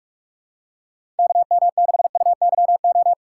Morse code, heard as a keyed tone, about 45 wpm, 700 Hz. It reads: KM6UYO
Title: USA Call Signs - 6 Characters - 6x Character Spacing - 45wpm